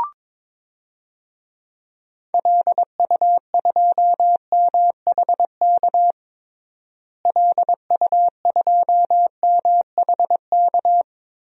Morse code, heard as a keyed tone, about 22 wpm, 700 Hz. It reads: LU2MHK LU2MHK
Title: DXCC-100 Call Signs - 6 Characters - Repeated 3x - 22wpm